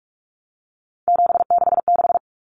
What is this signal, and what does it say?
Morse code, heard as a keyed tone, about 44 wpm, 700 Hz. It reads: 766
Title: Common Cut Numbers - 3-Digit 45wpm